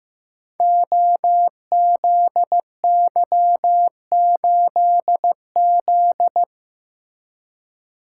Morse code, heard as a keyed tone, about 15 wpm, 700 Hz. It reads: OZY8Z
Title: Code Groups - 5 Letter:Numbers - 15wpm